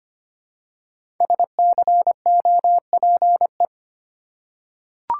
Morse code, heard as a keyed tone, about 25 wpm, 700 Hz. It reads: SCOPE
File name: 5 Letter Words - First 3 Letters Given - 25wpm